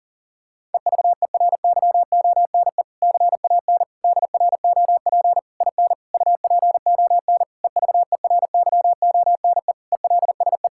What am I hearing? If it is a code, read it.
Morse code, heard as a keyed tone, about 40 wpm, 700 Hz. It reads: EVERYONE CAN DROP IN UPON EVERYONE ELSE
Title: Sentences from Top 1600 Words 40wpm